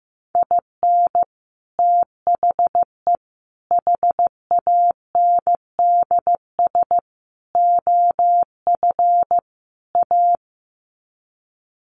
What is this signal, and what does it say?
Morse code, heard as a keyed tone, about 15 wpm, 700 Hz. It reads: IN THE HANDS OF A
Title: Sets of 5 Words Encore 15wpm